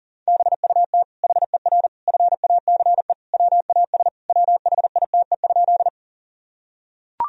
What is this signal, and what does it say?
Morse code, heard as a keyed tone, about 40 wpm, 700 Hz. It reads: BUT HER FACE WAS WHITE?
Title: Sentences from Top 1000 Words - Review Part 2 - 40wpm